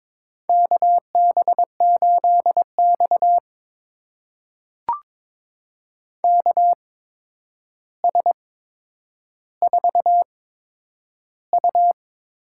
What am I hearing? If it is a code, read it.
Morse code, heard as a keyed tone, about 22 wpm, spaced as for 9 wpm, 700 Hz. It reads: KB8X KS4U
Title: USA Call Signs - 8x Character Spacing - 22wpm